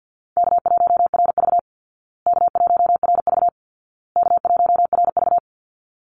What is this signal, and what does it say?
Morse code, heard as a keyed tone, about 48 wpm, 700 Hz. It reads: K1RV K1RV K1RV
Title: Easy Call Signs Repeated 3x - 50wpm